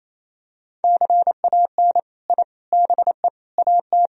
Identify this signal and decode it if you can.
Morse code, heard as a keyed tone, about 27 wpm, 700 Hz. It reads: CAN I BE AT
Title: Sentences from Top 1000 Words - Review Part 1 - 28wpm